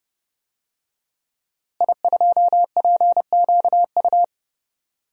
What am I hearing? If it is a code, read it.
Morse code, heard as a keyed tone, about 30 wpm, 700 Hz. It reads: I2PQU
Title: DXCC-100 Call Signs - 5 Characters - Repeated 3x - 30wpm